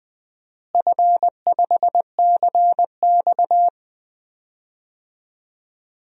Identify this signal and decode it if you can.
Morse code, heard as a keyed tone, about 20 wpm, 700 Hz. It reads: F5CX
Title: DXCC-100 Call Signs - 4 Characters - Repeated 2x - 20wpm